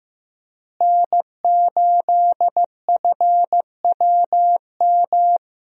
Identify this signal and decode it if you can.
Morse code, heard as a keyed tone, about 15 wpm, 700 Hz. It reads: N8FWM
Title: USA Call Signs - 5 Characters - Repeated 2x - 15wpm